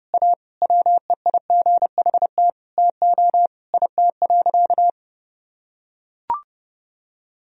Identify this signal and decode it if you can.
Morse code, heard as a keyed tone, about 30 wpm, 700 Hz. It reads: A WEIGHT TO IT.
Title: Sentences from Top 1000 Words - Review Part 3 - 30wpm